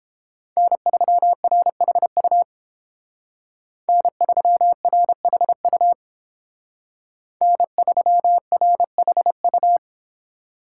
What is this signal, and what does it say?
Morse code, heard as a keyed tone, about 29 wpm, 700 Hz. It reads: N3RHU N3RHU N3RHU